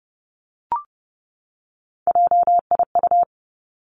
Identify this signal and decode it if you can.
Morse code, heard as a keyed tone, about 32 wpm, 700 Hz. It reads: JIU
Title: Three Challenging Letter-Number Combinations 30wpm